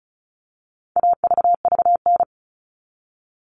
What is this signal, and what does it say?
Morse code, heard as a keyed tone, about 34 wpm, 700 Hz. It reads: AVVN
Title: All Cut Numbers - 4-Digit 35wpm